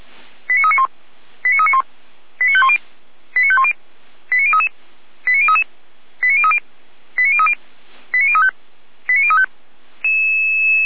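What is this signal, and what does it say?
Five-tone selective call (ZVEI), heard as quick runs of five tones, tones of about 70 ms. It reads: 89391 89391 89510 89510 89033 89033 89030 89030 89035 89035
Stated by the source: ZVEI five-tone call received by SDR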